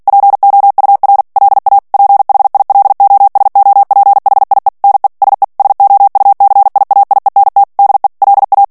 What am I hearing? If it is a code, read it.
Morse code, heard as a keyed tone, about 48 wpm, 800 Hz. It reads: POUR LA PHILOSOPHIE NE SE SOUCIAIENT DE LA
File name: CW / FICHIERS-MP3 / FRANCAIS TEXTE CLAIR 2 VITESSE 45 MOTS MINUTES